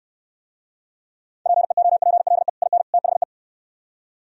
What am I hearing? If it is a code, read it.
Morse code, heard as a keyed tone, about 67 wpm, 700 Hz. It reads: PEOPLE IN THE